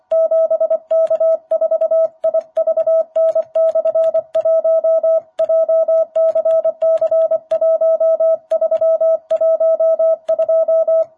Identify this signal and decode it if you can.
Morse code, heard as a keyed tone, about 25 wpm, 660 Hz. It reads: 7K4IVN/1 JCC1312
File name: jh8cbh / 2021cw